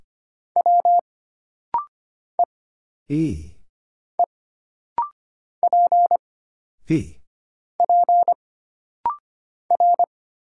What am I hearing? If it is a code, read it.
Morse code, heard as a keyed tone, about 25 wpm, 700 Hz. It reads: W E E P P R
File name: Single Character (Letter:Number:Punctuation:Prosign) - Rapid-Fire - 25wpm